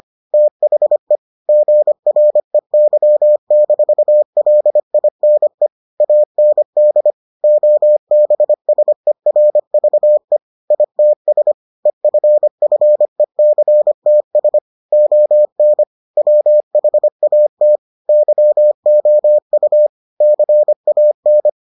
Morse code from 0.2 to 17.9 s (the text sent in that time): THE GREY-LINE AND OBSERVE ITS EFFECTS ON WHAT